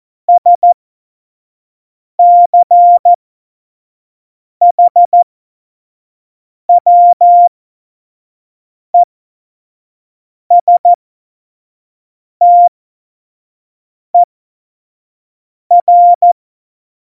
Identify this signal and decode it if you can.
Morse code, heard as a keyed tone, about 14 wpm, spaced as for 5 wpm, 700 Hz. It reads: SCHWESTER
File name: Morsekurs / Witze / Witz 29. September 2020, 25 F70